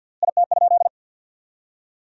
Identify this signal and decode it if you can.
Morse code, heard as a keyed tone, about 50 wpm, 700 Hz. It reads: IT?
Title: 3x Word-Spacing - Sentences from Top 400 Words - 50wpm